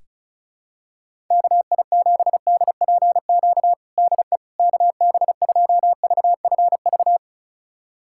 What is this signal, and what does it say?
Morse code, heard as a keyed tone, about 36 wpm, 700 Hz. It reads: KI7DPQ DE KB2VFV
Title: USA Callsign DE Callsign - 6 Characters - 35wpm